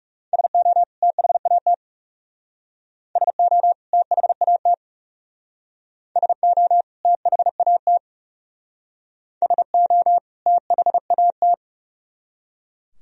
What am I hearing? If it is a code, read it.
Morse code, heard as a keyed tone, about 37 wpm, 700 Hz. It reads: SO THAT SO THAT SO THAT SO THAT
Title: Speed-Racing - Sets of 2 Words 30wpm